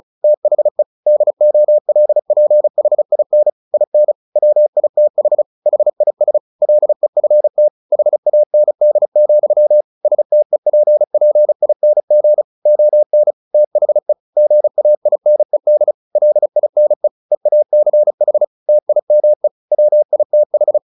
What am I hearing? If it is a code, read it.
Morse code, heard as a keyed tone, about 36 wpm, 600 Hz. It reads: THE DOLPHIN IN WITH HIS LEFT HAND, STEPPING ON THE GAINED LINE EACH TIME WITH
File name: the old man and the sea 2 35wpm